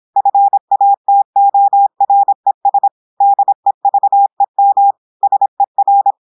Morse code, heard as a keyed tone, about 26 wpm, 800 Hz. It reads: FATORES DEVEM SER
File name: portugais texte en clair 2 vitesse 26 mots minutes